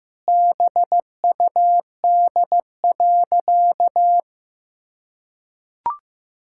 Morse code, heard as a keyed tone, about 15 wpm, 700 Hz. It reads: BUD.
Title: Pangram Challenge v2 - 15wpm